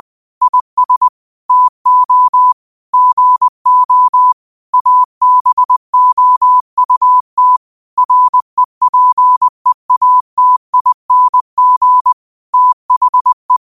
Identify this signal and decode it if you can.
Morse code, heard as a keyed tone, about 20 wpm, 1000 Hz. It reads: IS TO GO ABOUT REPEATING THE